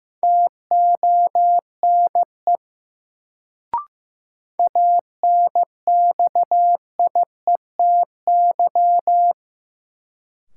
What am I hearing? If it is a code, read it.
Morse code, heard as a keyed tone, about 15 wpm, 700 Hz. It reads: TONE ANXIETY
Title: Sentences from Top 1600 Words - New Word Drill - 15wpm